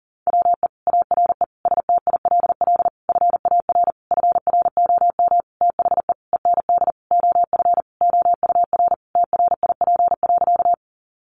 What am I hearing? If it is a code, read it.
Morse code, heard as a keyed tone, about 40 wpm, 700 Hz. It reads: WE ARE STILL FAR FROM THE END OF OUR TRIP.
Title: Sentences from Top 1300 Words 40wpm